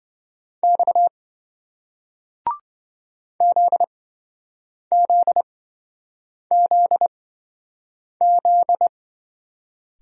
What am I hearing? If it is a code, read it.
Morse code, heard as a keyed tone, about 27 wpm, spaced as for 7 wpm, 700 Hz. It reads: X ZZZZ